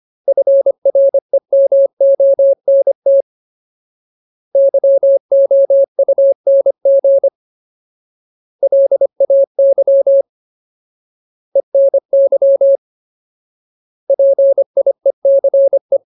Morse code, heard as a keyed tone, about 25 wpm, 550 Hz. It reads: FREMONT YOUNG LAY ENY PIECE
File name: Ultimate Word List / 25WPM Ultimate